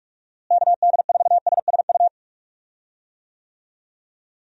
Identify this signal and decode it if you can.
Morse code, heard as a keyed tone, about 41 wpm, 700 Hz. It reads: KD4SSU
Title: USA Call Signs - 6 Characters - Repeated 3x - 45wpm